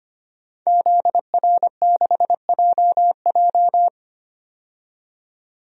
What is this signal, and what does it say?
Morse code, heard as a keyed tone, about 25 wpm, 700 Hz. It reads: ZR6JJ
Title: DXCC-100 Call Signs - 5 Characters - 25wpm